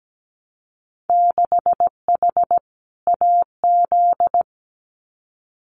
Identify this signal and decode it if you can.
Morse code, heard as a keyed tone, about 17 wpm, 700 Hz. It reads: 6H AZ
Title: Winter Field Day Exchanges Encore 17wpm